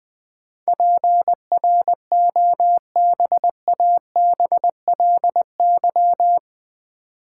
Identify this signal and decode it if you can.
Morse code, heard as a keyed tone, about 21 wpm, 700 Hz. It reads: PROBABLY
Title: Sentences from Top 700 Words - New Word Drill - 20wpm